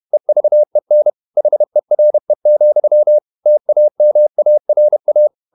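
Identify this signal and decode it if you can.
Morse code, heard as a keyed tone, about 31 wpm, 600 Hz. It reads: EVEN HERE, TAMARA
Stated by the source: Morse code at 30 wpm, the novel Never by Ken Follett in English, from CW med Gnister.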